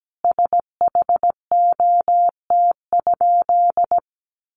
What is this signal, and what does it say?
Morse code, heard as a keyed tone, about 17 wpm, 700 Hz. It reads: SHOT?
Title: Sentences from Top 1100 Words 17wpm